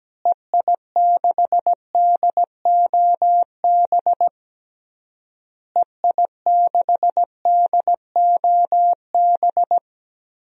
Morse code, heard as a keyed tone, about 17 wpm, 700 Hz. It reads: EI6DOB EI6DOB
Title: DXCC-100 Call Signs - 6 Characters - Repeated 3x - 17wpm